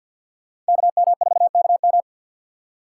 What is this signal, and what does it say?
Morse code, heard as a keyed tone, about 62 wpm, 700 Hz. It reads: KM4KM
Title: Speed Racing - USA Call Signs 50wpm